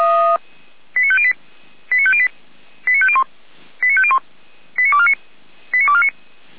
Five-tone selective call (ZVEI), heard as quick runs of five tones, tones of about 70 ms. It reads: 89508 89508 89501 89501 89249 89249